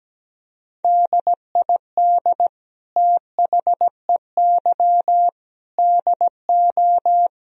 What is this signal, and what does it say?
Morse code, heard as a keyed tone, about 17 wpm, 700 Hz. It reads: DID THEY DO
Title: Sentences from Top 1000 Words - Review Part 4 - 17wpm